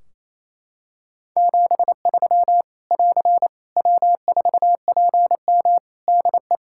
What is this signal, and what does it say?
Morse code, heard as a keyed tone, about 28 wpm, 700 Hz. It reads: 73 <AR> W4PM DE
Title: K1USN SST QSO Practice 28wpm